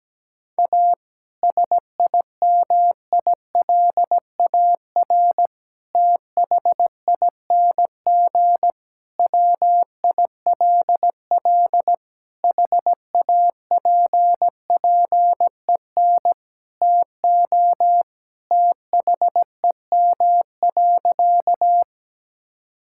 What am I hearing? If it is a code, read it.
Morse code, heard as a keyed tone, about 17 wpm, 700 Hz. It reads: A SIMILAR THING WILL HAPPEN TO THEM.
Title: Sentences from Top 1000 Words - Review Part 5 - 17wpm